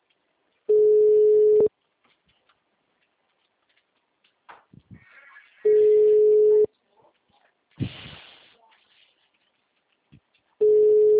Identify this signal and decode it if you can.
Ringback tone (Europe, 425 Hz).